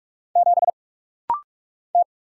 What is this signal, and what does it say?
Morse code, heard as a keyed tone, about 44 wpm, 700 Hz. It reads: 7 T
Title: Single Letter-Number Alt - Rapid-Fire 45wpm